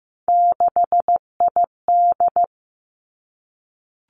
Morse code, heard as a keyed tone, about 15 wpm, 700 Hz. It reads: 6ID
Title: Three Challenging Letter-Number Combinations 15wpm